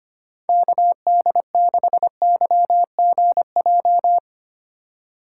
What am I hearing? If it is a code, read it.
Morse code, heard as a keyed tone, about 25 wpm, 700 Hz. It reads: KD6YGJ
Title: USA Call Signs - 6 Characters - 8x Character Spacing - 25wpm